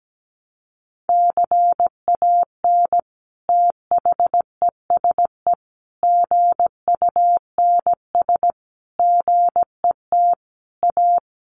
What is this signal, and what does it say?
Morse code, heard as a keyed tone, about 17 wpm, 700 Hz. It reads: CAN THESE GUNS GET A
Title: A Sound Of Thunder 17wpm